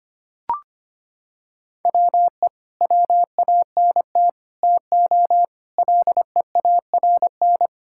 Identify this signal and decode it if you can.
Morse code, heard as a keyed tone, about 25 wpm, 700 Hz. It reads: WE WANT TO LEARN
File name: practice / 25wpm / Sentences with 3-Word Phrases - 25wpm